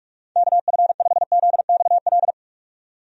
Morse code, heard as a keyed tone, about 46 wpm, 700 Hz. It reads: KF5ZXL
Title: USA Call Signs - 6 Characters - 45wpm